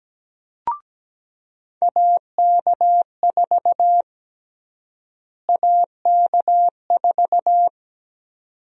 Morse code, heard as a keyed tone, about 17 wpm, 700 Hz. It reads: AK4 AK4
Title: USA Call Sign Prefixes - Repeated 2x - 17wpm